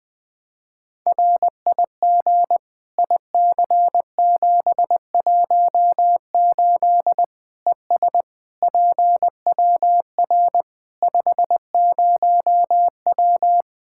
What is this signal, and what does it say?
Morse code, heard as a keyed tone, about 21 wpm, 700 Hz. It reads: RIG IC718 ES PWR 50W
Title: Practice QSO - Exchange 2 - Info - 20wpm